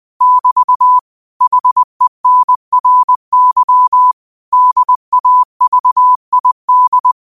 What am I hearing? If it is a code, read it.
Morse code, heard as a keyed tone, about 20 wpm, 1000 Hz. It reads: <BT> HENRY DAVID